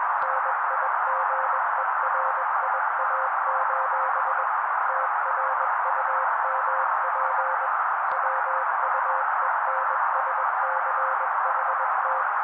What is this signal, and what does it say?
Morse code, heard as a keyed tone, about 21 wpm, 530 Hz. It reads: NIGERIA: TRUMP WUENSCHT
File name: daily / 20wpm noise